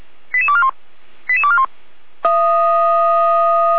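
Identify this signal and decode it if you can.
Five-tone selective call (ZVEI), heard as quick runs of five tones, tones of about 70 ms. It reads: 88241 88241
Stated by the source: ZVEI five-tone selective call received with an SDR.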